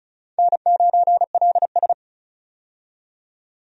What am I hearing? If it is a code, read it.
Morse code, heard as a keyed tone, about 34 wpm, 700 Hz. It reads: N9LS
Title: USA Call Signs - 4 Characters - 35wpm